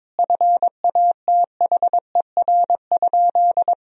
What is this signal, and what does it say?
Morse code, heard as a keyed tone, about 22 wpm, 700 Hz. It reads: FATHER?
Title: Sentences from Top 1000 Words - Review Part 2 - 22wpm